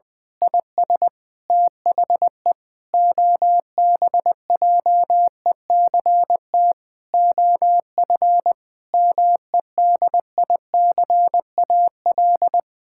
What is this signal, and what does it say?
Morse code, heard as a keyed tone, about 20 wpm, 700 Hz. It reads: IS THE OBJECT OF MEDICAL
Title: Sentences from Top 1000 Words - Review Part 6 - 20wpm